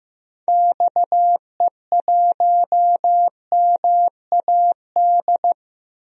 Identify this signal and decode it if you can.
Morse code, heard as a keyed tone, about 15 wpm, 700 Hz. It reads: XE1MAD
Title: DXCC-100 Call Signs - 6 Characters - 15wpm